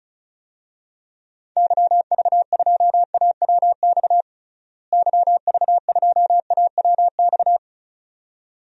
Morse code, heard as a keyed tone, about 36 wpm, 700 Hz. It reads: YV2AWX YV2AWX
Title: DXCC-100 Call Signs - 6 Characters - Repeated 2x - 35wpm